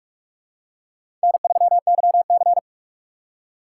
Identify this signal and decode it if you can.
Morse code, heard as a keyed tone, about 44 wpm, 700 Hz. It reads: N3YC
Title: USA Call Signs - 4 Characters - 6x Character Spacing - 45wpm